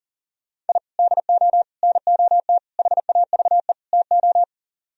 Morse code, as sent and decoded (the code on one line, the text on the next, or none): .. / -.. --- / -. --- - / .... .- ...- . / - ---
I DO NOT HAVE TO